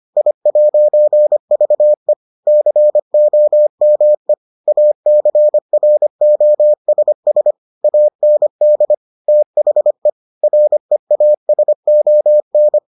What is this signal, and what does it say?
Morse code, heard as a keyed tone, about 25 wpm, 600 Hz. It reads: I'VE COME ACROSS AND THE REASON